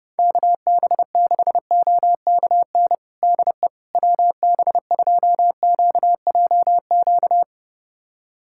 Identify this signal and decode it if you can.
Morse code, heard as a keyed tone, about 30 wpm, 700 Hz. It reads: KB6OKN DE WB2QJQ
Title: USA Callsign DE Callsign - 6 Characters - 30wpm